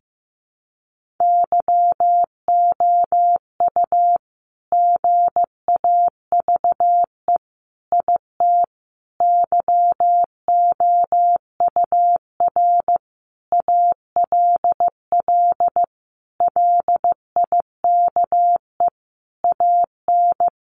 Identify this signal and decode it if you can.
Morse code, heard as a keyed tone, about 15 wpm, 700 Hz. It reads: YOU GAVE IT YOUR ALL LIKE AN
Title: Sentences from Top 1500 Words 15wpm